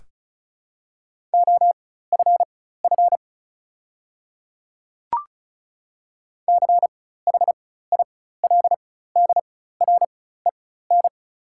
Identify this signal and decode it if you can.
Morse code, heard as a keyed tone, about 36 wpm, spaced as for 16 wpm, 690 Hz. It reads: OFF CHILDREN
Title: 4x Character Spacing - Top 300 Words - 35wpm